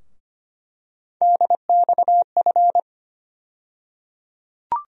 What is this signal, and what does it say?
Morse code, heard as a keyed tone, about 25 wpm, 700 Hz. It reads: DXF
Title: USA Call Sign Postfixes - 3 Letters 25wpm